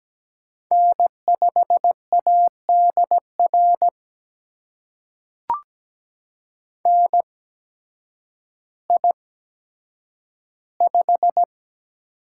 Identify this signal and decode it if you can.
Morse code, heard as a keyed tone, about 17 wpm, spaced as for 9 wpm, 700 Hz. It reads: N5ADR NI5
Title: USA Call Signs - 8x Character Spacing - 17wpm